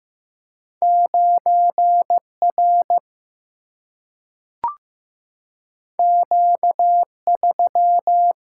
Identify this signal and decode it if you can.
Morse code, heard as a keyed tone, about 15 wpm, 700 Hz. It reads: 9R Q3
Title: Single-Letter Single-Number 15wpm